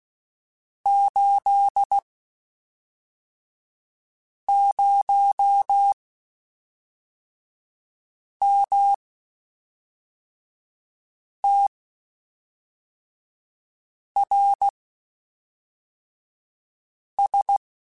Morse code, heard as a keyed tone, about 16 wpm, spaced as for 3 wpm, 800 Hz. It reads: 80MTRS
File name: QSOs / QSOs-mp3s / WD0FBT 1@5